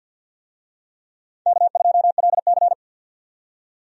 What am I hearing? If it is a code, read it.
Morse code, heard as a keyed tone, about 50 wpm, 700 Hz. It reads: K2LC
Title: USA Call Signs - 4 Characters - 50wpm